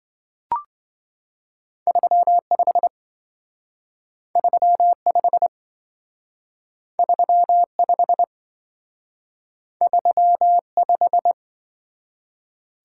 Morse code, heard as a keyed tone, about 26 wpm, 700 Hz. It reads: 35 35 35 35